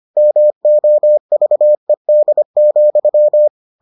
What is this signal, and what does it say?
Morse code, heard as a keyed tone, about 25 wpm, 600 Hz. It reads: MOVED,